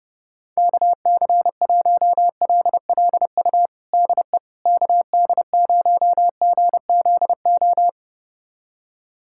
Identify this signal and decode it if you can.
Morse code, heard as a keyed tone, about 32 wpm, 700 Hz. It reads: KC1LLU DE KD0GZO
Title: USA Callsign DE Callsign 30wpm